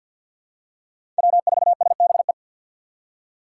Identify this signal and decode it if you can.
Morse code, heard as a keyed tone, about 52 wpm, 700 Hz. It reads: W4SBE
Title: USA Call Signs - 5 Characters - 6x Character Spacing - 50wpm